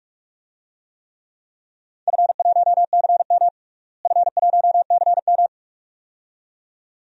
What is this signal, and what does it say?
Morse code, heard as a keyed tone, about 46 wpm, 700 Hz. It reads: F1CM F1CM
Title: DXCC-100 Call Signs - 4 Characters - Repeated 2x - 45wpm